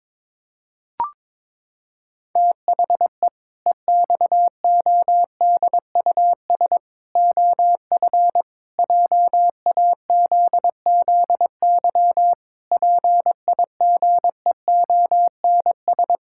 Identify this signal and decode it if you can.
Morse code, heard as a keyed tone, about 21 wpm, 700 Hz. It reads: THE EXODUS OF JAZZY PIGEONS
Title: Pangram Challenge v2 - 22wpm